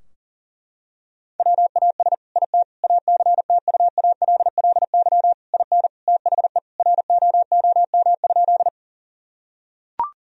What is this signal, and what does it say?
Morse code, heard as a keyed tone, about 40 wpm, 700 Hz. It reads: WAS IT ACTUALLY IN THE ROOM?
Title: Sentences from Top 1000 Words - Review Part 4 - 40wpm